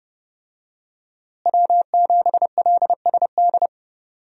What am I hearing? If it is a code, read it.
Morse code, heard as a keyed tone, about 30 wpm, 700 Hz. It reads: W7LSD
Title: USA Call Signs - 5 Characters - Repeated 3x - 30wpm